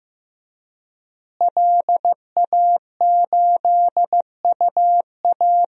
Morse code, heard as a keyed tone, about 15 wpm, 700 Hz. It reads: LA8UA
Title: DXCC-100 Call Signs - 5 Characters - Repeated 3x - 15wpm